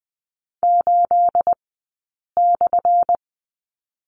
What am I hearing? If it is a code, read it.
Morse code, heard as a keyed tone, about 20 wpm, 700 Hz. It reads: 8 /
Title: 217 Lesson 35d - Number 0 - Two Characters 20wpm